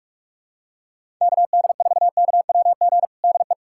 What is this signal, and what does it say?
Morse code, heard as a keyed tone, about 46 wpm, 700 Hz. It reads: KD4KWG DE
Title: USA Callsign DE Callsign 45wpm